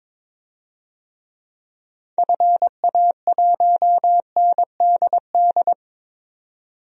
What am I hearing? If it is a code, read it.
Morse code, heard as a keyed tone, about 22 wpm, 700 Hz. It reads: FA1NDD
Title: DXCC-100 Call Signs - 6 Characters - 22wpm